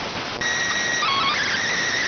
SSTV (slow-scan television), PD 290.